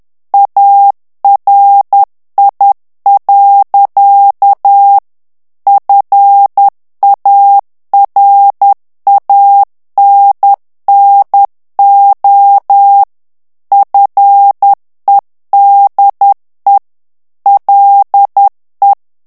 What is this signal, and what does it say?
Morse code, heard as a keyed tone, about 10 wpm, 800 Hz. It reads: ARI. FARANNO FEDE LE
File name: ITALIEN texte clair 2 vitesse 10 mots minute